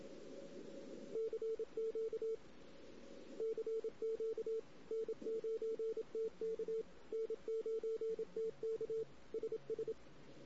Morse code, heard as a keyed tone, about 27 wpm, 450 Hz. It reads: CQ CQ N9TK N9TK SS
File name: CW / 2008-ss-cw-k5tr-op-n5rz -16:00